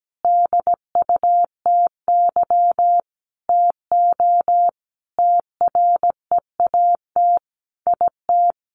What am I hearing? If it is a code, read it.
Morse code, heard as a keyed tone, about 17 wpm, 700 Hz. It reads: DUTY TO TREAT IT
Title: Sentences from Top 1500 Words 17wpm